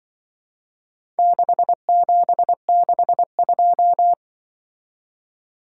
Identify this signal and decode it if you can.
Morse code, heard as a keyed tone, about 25 wpm, 700 Hz. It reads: 6762
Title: Speed-Racing - Four Letter-Number Combinations - 20wpm